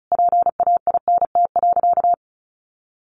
Morse code, heard as a keyed tone, about 34 wpm, 700 Hz. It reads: PAINT.
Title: A Sound Of Thunder 35wpm